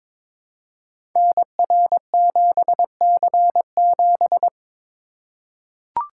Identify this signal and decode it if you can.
Morse code, heard as a keyed tone, about 22 wpm, 700 Hz. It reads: NR7C7
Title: Code Groups - 5 Letter:Numbers - 22wpm